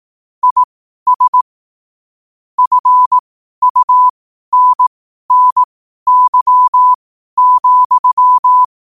Morse code, heard as a keyed tone, about 18 wpm, spaced as for 12 wpm, 1000 Hz. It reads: IS FUNNY,
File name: qotdfiles / 12wpm